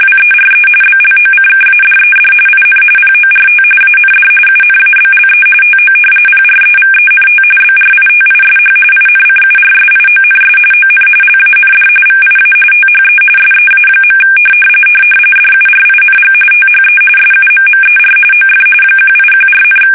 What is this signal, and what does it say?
RTTY, about 45.45 baud. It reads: WERE BLOCKADED IN PORT AND UNABLE TO PREVENT BRITISH INCURSIONS ON LAND VIA THE SEA. DURING THE SUMMER OF 1814,
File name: fsk850 / usnavy2